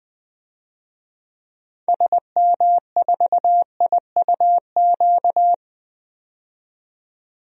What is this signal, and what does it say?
Morse code, heard as a keyed tone, about 21 wpm, 700 Hz. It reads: SM4IUQ
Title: DXCC-100 Call Signs - 6 Characters - 20wpm